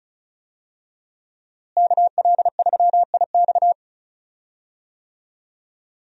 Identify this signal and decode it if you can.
Morse code, heard as a keyed tone, about 34 wpm, 700 Hz. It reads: KL3IX
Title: USA Call Signs - 5 Characters - Repeated 3x - 35wpm